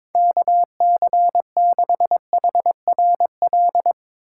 Morse code, heard as a keyed tone, about 22 wpm, 700 Hz. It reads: KC6HRL
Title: USA Call Signs - 4x Character Spacing - 22wpm